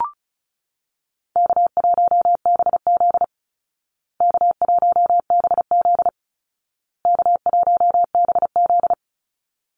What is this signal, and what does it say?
Morse code, heard as a keyed tone, about 34 wpm, 700 Hz. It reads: K1BZ K1BZ K1BZ
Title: Easy Call Signs Repeated 3x - 35wpm